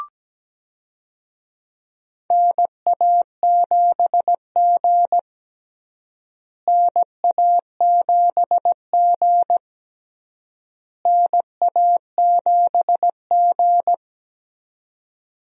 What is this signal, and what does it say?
Morse code, heard as a keyed tone, about 17 wpm, 700 Hz. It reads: NA7G NA7G NA7G
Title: USA Call Signs - 4 Characters - Repeated 3x - 17wpm